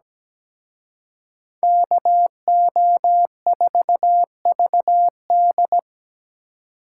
Morse code, heard as a keyed tone, about 20 wpm, 700 Hz. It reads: KO4VD
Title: Speed Racing - USA 2x2 Call Signs 17wpm